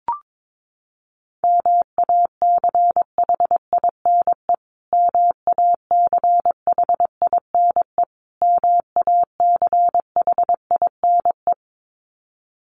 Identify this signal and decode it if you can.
Morse code, heard as a keyed tone, about 22 wpm, 700 Hz. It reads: MACHINE MACHINE MACHINE
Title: Top 500 Words Repeated 22wpm